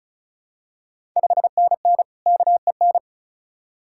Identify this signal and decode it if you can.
Morse code, heard as a keyed tone, about 34 wpm, 690 Hz. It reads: 5NN KEN
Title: Ontario QSO Party Exchange 35wpm